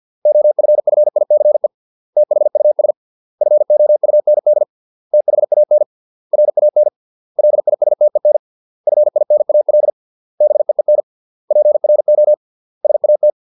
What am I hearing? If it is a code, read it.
Morse code, heard as a keyed tone, about 50 wpm, 600 Hz. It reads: OFFICE THUS FOUND THAN RAN LISTEN FINAL BEEN PRO SAT